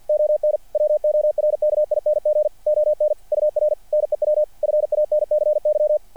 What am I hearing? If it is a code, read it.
Morse code, heard as a keyed tone, about 50 wpm, 600 Hz. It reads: ON WORKING ON UR NEW FANCY
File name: archived / CRSnet